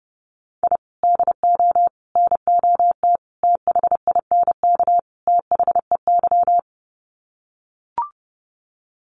Morse code, heard as a keyed tone, about 28 wpm, 700 Hz. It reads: I DO NOT THINK THEY
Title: Sets of 5 Words 30wpm